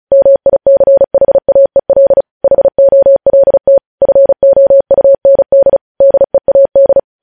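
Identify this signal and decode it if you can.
Morse code, heard as a keyed tone, about 36 wpm, 550 Hz. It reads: MICHAEL HOLT FOUND DEAD
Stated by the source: Morse code practice reading news headlines at 35 words per minute.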